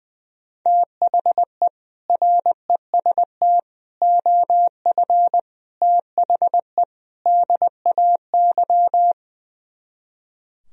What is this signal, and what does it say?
Morse code, heard as a keyed tone, about 21 wpm, 700 Hz. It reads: THE REST OF THE DAY
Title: Speed-Racing - Sets of 5 Words 20wpm